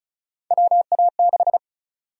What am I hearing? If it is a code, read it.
Morse code, heard as a keyed tone, about 36 wpm, 700 Hz. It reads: WA6
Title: USA Call Sign Prefixes - 35wpm